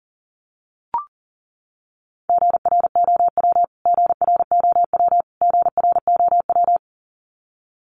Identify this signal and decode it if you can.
Morse code, heard as a keyed tone, about 40 wpm, 700 Hz. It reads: GROW GROW GROW
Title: Top 500 Words Repeated 40wpm